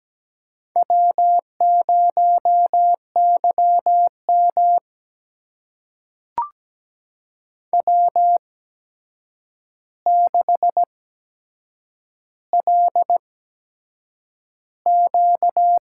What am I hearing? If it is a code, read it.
Morse code, heard as a keyed tone, about 17 wpm, spaced as for 7 wpm, 700 Hz. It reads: W0YM W6LQ
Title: USA Call Signs - 4 Characters - 8x Character Spacing - 17wpm